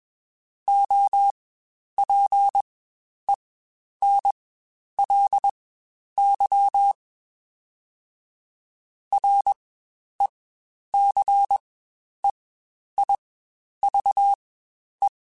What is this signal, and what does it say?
Morse code, heard as a keyed tone, about 21 wpm, spaced as for 10 wpm, 800 Hz. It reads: OPENLY RECEIVE
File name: Quotes Spaced / Quotes Spaced-mp3s / Advice sp @20